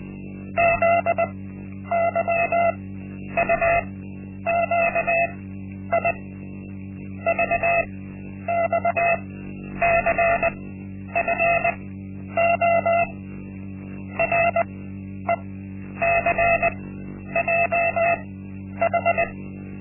Morse code, heard as a keyed tone, about 20 wpm, spaced as for 10 wpm, 680 Hz. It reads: ZYUQI VXCFO RECJH
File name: Linksmakalnis-1